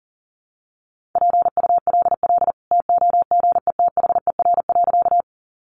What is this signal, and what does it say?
Morse code, heard as a keyed tone, about 40 wpm, 700 Hz. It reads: PULL TOGETHER.
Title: Sentences from Top 400 Words 40wpm